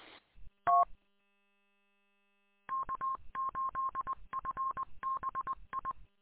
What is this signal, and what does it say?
Morse code, heard as a keyed tone, about 21 wpm, 1070 Hz. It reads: T K8FBI
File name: k8fbi-43732-combined